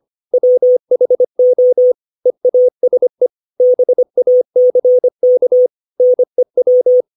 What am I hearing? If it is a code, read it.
Morse code, heard as a keyed tone, about 25 wpm, 500 Hz. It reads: WHO EASE BACK NEW
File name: 632 words / 632 Words 25WPM